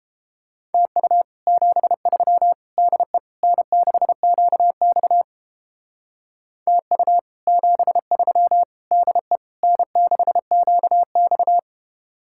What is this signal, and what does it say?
Morse code, heard as a keyed tone, about 31 wpm, 700 Hz. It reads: TU 73 DE N6QX TU 73 DE N6QX